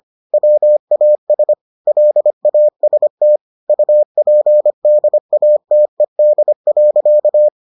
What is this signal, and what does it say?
Morse code, heard as a keyed tone, about 25 wpm, 600 Hz. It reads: WAS LAST UPDATED.